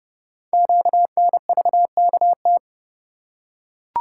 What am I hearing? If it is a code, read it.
Morse code, heard as a keyed tone, about 30 wpm, 700 Hz. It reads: QNVKT
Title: Code Groups - 5 Letter:Numbers - 30wpm